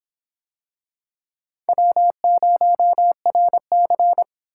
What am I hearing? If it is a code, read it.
Morse code, heard as a keyed tone, about 24 wpm, 700 Hz. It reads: W0RC